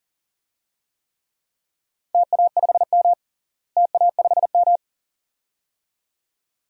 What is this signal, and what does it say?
Morse code, heard as a keyed tone, about 37 wpm, 700 Hz. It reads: TA5M TA5M
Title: DXCC-100 Call Signs - 4 Characters - Repeated 2x - 40wpm